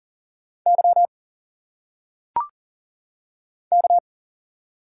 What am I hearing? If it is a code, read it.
Morse code, heard as a keyed tone, about 40 wpm, 700 Hz. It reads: Y K
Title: Single Letters - Phonetic Pronunciation 40wpm